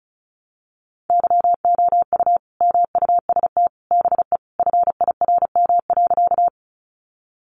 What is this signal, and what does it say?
Morse code, heard as a keyed tone, about 34 wpm, 700 Hz. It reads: YOU MUST BE FIRM.
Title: Sentences from Top 1400 Words 35wpm